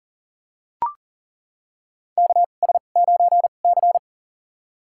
Morse code, heard as a keyed tone, about 40 wpm, spaced as for 29 wpm, 700 Hz. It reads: KS9C
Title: USA Call Signs - 4 Characters - 2x Character Spacing - 40wpm